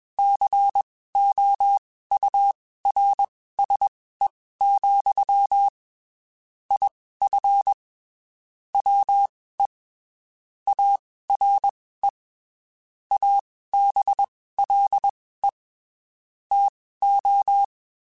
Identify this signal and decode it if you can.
Morse code, heard as a keyed tone, about 21 wpm, spaced as for 15 wpm, 800 Hz. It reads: COURSE, IF WE ARE ABLE TO
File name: (spaced) / QSOs Spaced-mp3s / WA7QOX 1 @20 sp